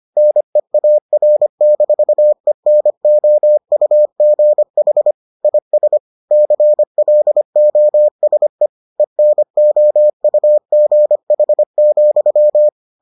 Morse code, heard as a keyed tone, about 25 wpm, 600 Hz. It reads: NEAR-ENOUGH IS CLOSE ENOUGH,